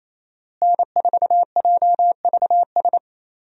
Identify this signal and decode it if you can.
Morse code, heard as a keyed tone, about 28 wpm, 700 Hz. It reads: N4JVS
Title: USA Call Signs - 5 Characters - Repeated 3x - 28wpm